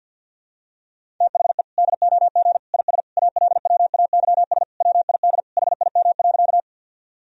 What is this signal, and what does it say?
Morse code, heard as a keyed tone, about 51 wpm, 700 Hz. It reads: THE DOG IS ALWAYS WID HIM.
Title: Sentences from Top 400 Words and Abbreviations 50wpm